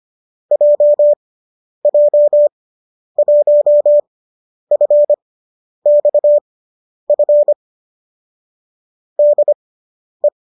Morse code, heard as a keyed tone, about 25 wpm, spaced as for 10 wpm, 590 Hz. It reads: JJ1FXF DE